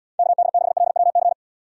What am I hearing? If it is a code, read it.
Morse code, heard as a keyed tone, about 75 wpm, 700 Hz. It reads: KD6LWC